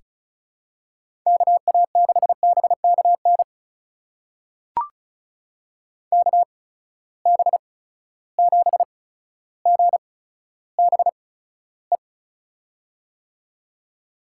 Morse code, heard as a keyed tone, about 36 wpm, spaced as for 15 wpm, 700 Hz. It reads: KA6BKN KB7GBE
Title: USA Call Signs - 6 Characters - 8x Character Spacing - 35wpm